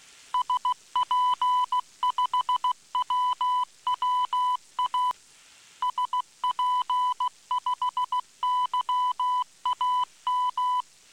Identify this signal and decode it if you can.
Morse code, heard as a keyed tone, about 16 wpm, 1000 Hz. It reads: SP5WWA SP5YAM